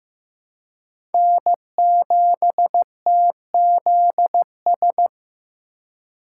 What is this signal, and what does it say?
Morse code, heard as a keyed tone, about 15 wpm, 700 Hz. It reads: N7TZS
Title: USA Call Signs - 5 Characters - 2x Character Spacing - 15wpm